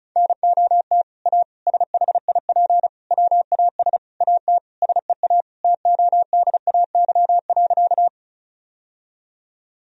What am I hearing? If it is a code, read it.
Morse code, heard as a keyed tone, about 36 wpm, 700 Hz. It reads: NOT A SHIP WAS AT SEA TODAY.
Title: Sentences from Top 1000 Words - Review Part 2 - 35wpm